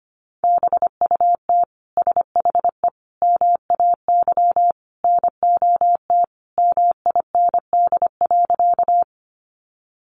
Morse code, heard as a keyed tone, about 25 wpm, 700 Hz. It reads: BUT SHE MAY NOT MIND.
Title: Sentences from Top 600 Words 25wpm